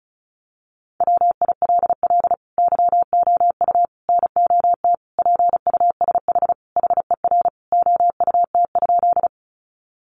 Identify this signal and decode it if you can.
Morse code, heard as a keyed tone, about 34 wpm, 700 Hz. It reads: WILL YOU NOT PUSH HER OUT?
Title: Sentences from Top 1300 Words 35wpm